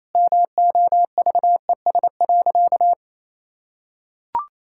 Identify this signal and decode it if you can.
Morse code, heard as a keyed tone, about 28 wpm, 700 Hz. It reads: MOVES.
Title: Pangram Challenge v2 - 28wpm